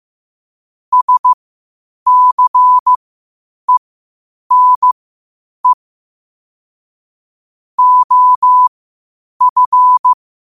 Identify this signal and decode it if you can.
Morse code, heard as a keyed tone, about 15 wpm, spaced as for 8 wpm, 1000 Hz. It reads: SCENE OF
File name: qotdfiles / 8wpm